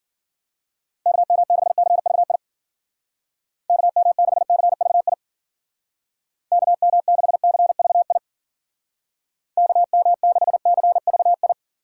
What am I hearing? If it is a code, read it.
Morse code, heard as a keyed tone, about 53 wpm, 700 Hz. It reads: KM6CVI KM6CVI KM6CVI KM6CVI